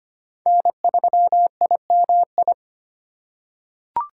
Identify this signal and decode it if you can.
Morse code, heard as a keyed tone, about 25 wpm, 700 Hz. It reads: N3IMI
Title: USA Call Signs - 5 Characters - 4x Character Spacing - 25wpm